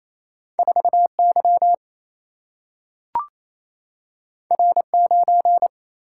Morse code, heard as a keyed tone, about 28 wpm, 700 Hz. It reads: 4 Y R 9
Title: Single-Letter Single-Number 28wpm